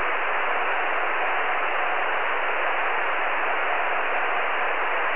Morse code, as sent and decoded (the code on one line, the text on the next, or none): none